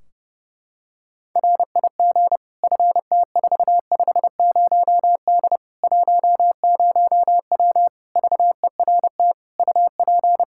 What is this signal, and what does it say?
Morse code, heard as a keyed tone, about 30 wpm, 700 Hz. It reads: RIG FT450D 10W VERT UP
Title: Practice QSO - Exchange 2 - Info - 30wpm